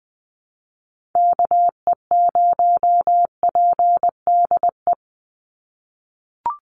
Morse code, heard as a keyed tone, about 20 wpm, 700 Hz. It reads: KE0PDE
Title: USA Call Signs 20wpm